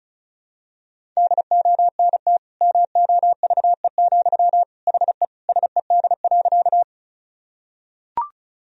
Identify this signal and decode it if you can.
Morse code, heard as a keyed tone, about 36 wpm, 700 Hz. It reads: DONT MOVE, HE SED.
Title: Sentences from Top 300 Words and Abbreviations 35wpm